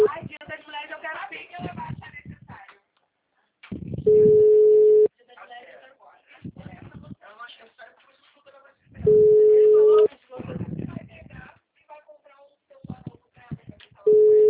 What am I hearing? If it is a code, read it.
Ringback tone (Europe, 425 Hz), with speech.
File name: out- -2003- - - .39456